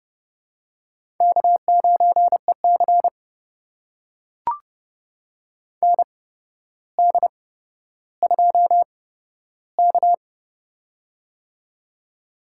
Morse code, heard as a keyed tone, about 30 wpm, spaced as for 13 wpm, 700 Hz. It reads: K9EC ND2K
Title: USA Call Signs - 8x Character Spacing - 30wpm